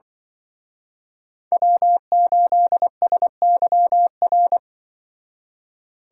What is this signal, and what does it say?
Morse code, heard as a keyed tone, about 24 wpm, 700 Hz. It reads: W8SYR
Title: Speed Racing - USA 1x3 Call Signs 20wpm